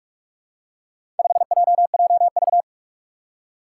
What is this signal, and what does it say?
Morse code, heard as a keyed tone, about 46 wpm, 700 Hz. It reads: 5JJV
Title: Four Challenging Letter-Number Combinations 45wpm